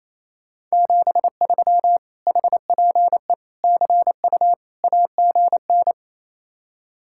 Morse code, heard as a keyed tone, about 33 wpm, 700 Hz. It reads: 73 HPE CU AGN
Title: Speed-Racing Common QSO Phrases Encore - 28wpm